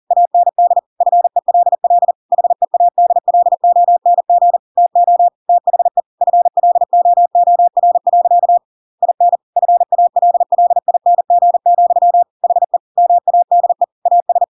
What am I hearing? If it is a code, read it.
Morse code, heard as a keyed tone, about 40 wpm, 700 Hz. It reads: AND FELL HEADLONG TO THE FLOOR. IN FALLING, HE MADE AS
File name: Pinocchio / Pinocchio ch7 40 40 ik1hge